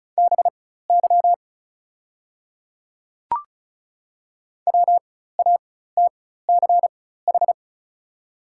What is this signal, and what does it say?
Morse code, heard as a keyed tone, about 34 wpm, spaced as for 16 wpm, 700 Hz. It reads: BY WATCH
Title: 4x Character Spacing - Top 300 Words - 35wpm